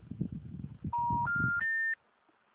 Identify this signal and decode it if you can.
Special information tone.
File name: out- -2007- - - .48787